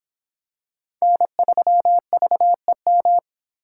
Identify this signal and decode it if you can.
Morse code, heard as a keyed tone, about 24 wpm, 700 Hz. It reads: N3VEM